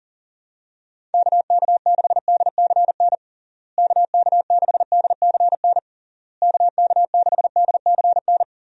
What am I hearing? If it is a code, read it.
Morse code, heard as a keyed tone, about 40 wpm, 700 Hz. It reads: KK6DCN KK6DCN KK6DCN
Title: USA Call Signs - 6 Characters - Repeated 3x - 40wpm